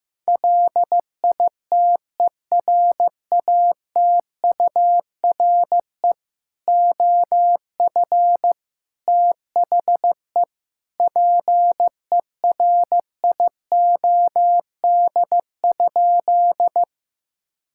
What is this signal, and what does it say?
Morse code, heard as a keyed tone, about 15 wpm, 700 Hz. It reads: LITERATURE OF THE PERIOD?
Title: Sentences from Top 1000 Words - Review Part 5 - 15wpm